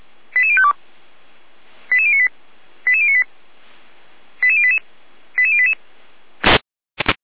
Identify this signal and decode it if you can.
Five-tone selective call (ZVEI), heard as quick runs of five tones, tones of about 70 ms. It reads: 88052 88098 88098 88088 88088